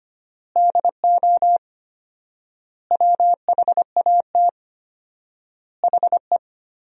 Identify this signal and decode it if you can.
Morse code, heard as a keyed tone, about 25 wpm, 700 Hz. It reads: DO WHAT HE
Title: 4x Word-Spacing - Sentences from Top 100 Words - 25wpm